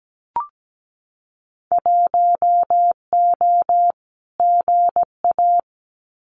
Morse code, heard as a keyed tone, about 17 wpm, 700 Hz. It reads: 1O GA
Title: Winter Field Day Exchanges 17wpm